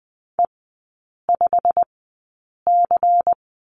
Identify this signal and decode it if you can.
Morse code, heard as a keyed tone, about 20 wpm, spaced as for 8 wpm, 700 Hz. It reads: E5C
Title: 093 Lesson 16e - Letter c - Three Characters 20wpm